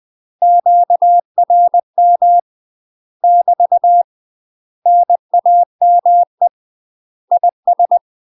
Morse code, heard as a keyed tone, about 20 wpm, 700 Hz. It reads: QRM <BT> NAME IS